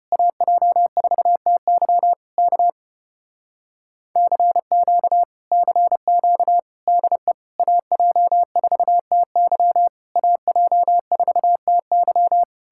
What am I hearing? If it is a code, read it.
Morse code, heard as a keyed tone, about 33 wpm, 700 Hz. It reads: AJ4TY K CQ CQ DE AJ4TY AJ4TY